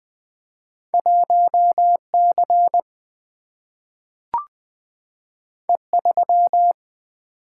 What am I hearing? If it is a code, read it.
Morse code, heard as a keyed tone, about 20 wpm, 700 Hz. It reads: 1C E3
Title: Two Letter-Number Combinations - 20wpm